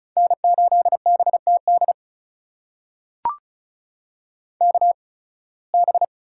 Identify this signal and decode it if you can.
Morse code, heard as a keyed tone, about 36 wpm, spaced as for 23 wpm, 700 Hz. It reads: N8BTD KB
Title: USA Call Signs - 5 Characters - 8x Character Spacing - 35wpm